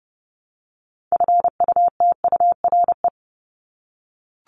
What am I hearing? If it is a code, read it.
Morse code, heard as a keyed tone, about 30 wpm, 700 Hz. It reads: FUTURE
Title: Top 501-750 Words 30wpm